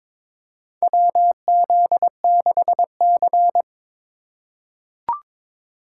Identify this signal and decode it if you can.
Morse code, heard as a keyed tone, about 22 wpm, 700 Hz. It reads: WZ6C
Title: USA Call Signs - 4 Characters - 2x Character Spacing - 22wpm